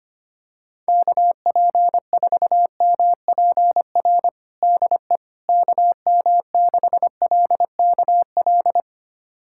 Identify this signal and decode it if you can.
Morse code, heard as a keyed tone, about 25 wpm, 700 Hz. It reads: KP4MPR DE KM6LKL
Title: USA Callsign DE Callsign - 6 Characters - 25wpm